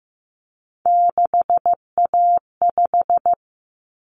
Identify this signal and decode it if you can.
Morse code, heard as a keyed tone, about 15 wpm, 700 Hz. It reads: 6A5
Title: Common Cut Numbers - 3-Digit 15wpm